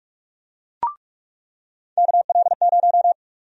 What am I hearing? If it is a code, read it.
Morse code, heard as a keyed tone, about 44 wpm, 700 Hz. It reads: KL0
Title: USA Call Sign Prefixes - 45wpm